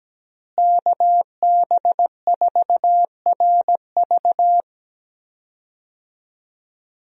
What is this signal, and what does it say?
Morse code, heard as a keyed tone, about 17 wpm, 700 Hz. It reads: KB4RV
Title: USA Call Signs - 5 Characters - Repeated 3x - 17wpm